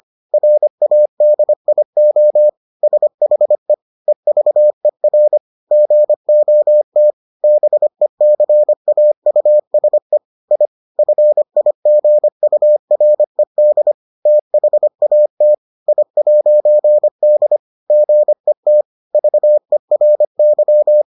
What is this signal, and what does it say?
Morse code, heard as a keyed tone, about 25 wpm, 600 Hz. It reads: RADIO SHE EVER GOT BECAUSE I FIGURED THAT I'D GET VERY